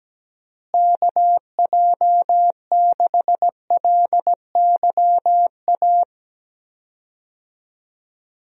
Morse code, heard as a keyed tone, about 17 wpm, 700 Hz. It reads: KJ6LYA
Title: USA Call Signs - 6 Characters - Repeated 2x - 17wpm